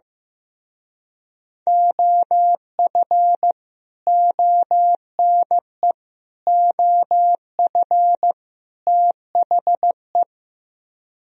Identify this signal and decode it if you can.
Morse code, heard as a keyed tone, about 15 wpm, 700 Hz. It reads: OF ONE OF THE
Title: Speed-Racing - Sets of 4 Words 15wpm